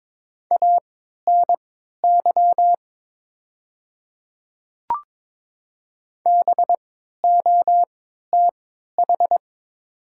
Morse code, heard as a keyed tone, about 22 wpm, spaced as for 12 wpm, 700 Hz. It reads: ANY BOTH
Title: 3x Character Spacing - Top 300 Words - 22wpm